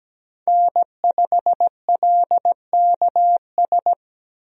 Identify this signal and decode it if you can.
Morse code, heard as a keyed tone, about 17 wpm, 700 Hz. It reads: N5LKS
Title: USA Call Signs - 5 Characters - 17wpm